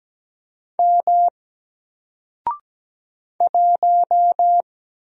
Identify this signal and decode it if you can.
Morse code, heard as a keyed tone, about 17 wpm, 700 Hz. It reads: M 1
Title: Single Letter-Number 17wpm